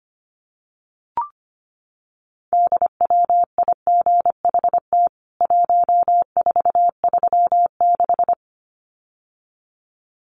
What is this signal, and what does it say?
Morse code, heard as a keyed tone, about 25 wpm, 700 Hz. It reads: DWIGHT 1436
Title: CWT Exchange 25wpm